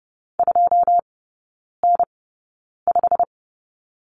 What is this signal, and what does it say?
Morse code, heard as a keyed tone, about 30 wpm, spaced as for 9 wpm, 700 Hz. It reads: 2N5
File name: course / 30:10wpm / 087 Lesson 15e - Number 5 - Three Characters 30wpm